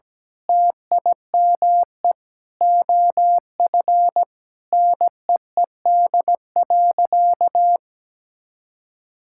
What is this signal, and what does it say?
Morse code, heard as a keyed tone, about 17 wpm, 700 Hz. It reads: TIME OF NEED.
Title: Sentences from Top 1000 Words - Review Part 5 - 17wpm